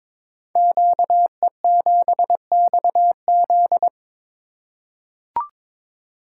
Morse code, heard as a keyed tone, about 22 wpm, 700 Hz. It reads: QE7XZ
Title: Code Groups - 5 Letter:Numbers - 22wpm